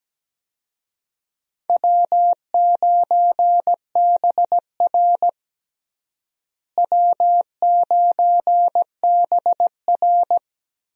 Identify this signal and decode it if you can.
Morse code, heard as a keyed tone, about 17 wpm, 700 Hz. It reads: W9BR W9BR
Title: USA Call Signs - 4 Characters - Repeated 3x - 17wpm